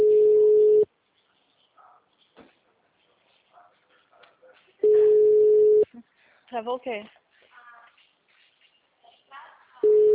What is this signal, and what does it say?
Ringback tone (Europe, 425 Hz), with speech.